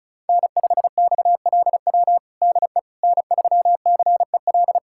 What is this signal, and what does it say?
Morse code, heard as a keyed tone, about 34 wpm, 700 Hz. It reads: N5XLW DE N3CEL
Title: USA Callsign DE Callsign 35wpm